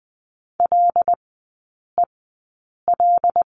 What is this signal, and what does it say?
Morse code, heard as a keyed tone, about 20 wpm, spaced as for 8 wpm, 700 Hz. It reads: LEL
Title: 074 Lesson 13e - Letter L - Three Characters 20wpm